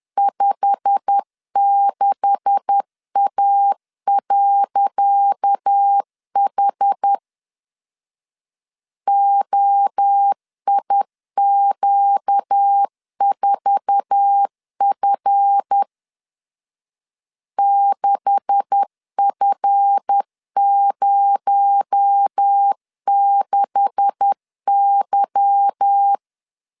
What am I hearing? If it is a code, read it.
Morse code, heard as a keyed tone, about 10 wpm, 800 Hz. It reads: 56A.H OIQ4F 6F06Y
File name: LEZIONI / 10 SIGNS / 10C DOT 10wpm 2940